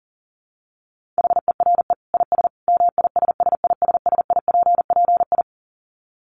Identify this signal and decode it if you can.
Morse code, heard as a keyed tone, about 40 wpm, 700 Hz. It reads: HERE IS MISSISSIPPI
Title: Dit Challenge 40wpm